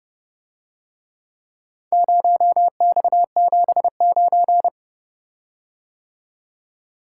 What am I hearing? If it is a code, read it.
Morse code, heard as a keyed tone, about 30 wpm, 700 Hz. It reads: 0X79